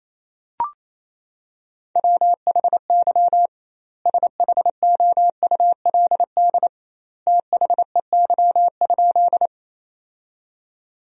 Morse code, heard as a keyed tone, about 28 wpm, 700 Hz. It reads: WHY SHOULD THEY?
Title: 2x Word-Spacing - Sentences from Top 300 Words - 28wpm